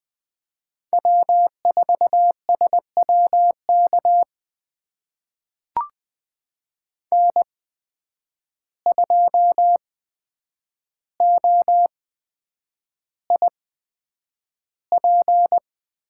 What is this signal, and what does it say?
Morse code, heard as a keyed tone, about 20 wpm, spaced as for 9 wpm, 700 Hz. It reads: W4SWK N2OIP
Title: USA Call Signs - 5 Characters - 8x Character Spacing - 20wpm